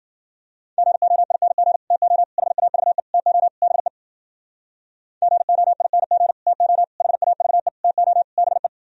Morse code, heard as a keyed tone, about 54 wpm, 700 Hz. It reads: GOING TO HAVE TO BE GOING TO HAVE TO BE